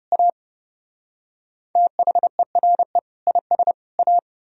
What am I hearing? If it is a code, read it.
Morse code, heard as a keyed tone, about 32 wpm, 700 Hz. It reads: A THERE IS A